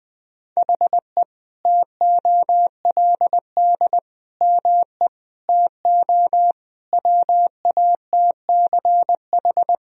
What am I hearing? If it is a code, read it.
Morse code, heard as a keyed tone, about 20 wpm, 700 Hz. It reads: HE TOLD ME TO WATCH
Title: Sentences from Top 1000 Words - Review Part 1 - 20wpm